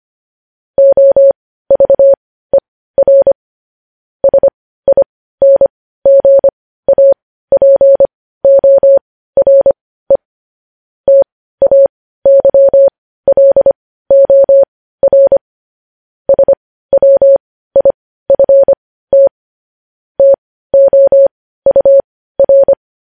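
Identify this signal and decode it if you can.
Morse code, heard as a keyed tone, about 25 wpm, spaced as for 15 wpm, 550 Hz. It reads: OVER SINGAPORE TAYLOR SWIFT TOUR
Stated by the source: Morse practice sending news headlines at 15 wpm.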